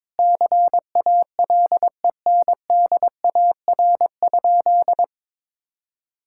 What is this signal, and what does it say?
Morse code, heard as a keyed tone, about 21 wpm, 700 Hz. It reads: CALENDAR?
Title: Sentences from Top 1600 Words 22wpm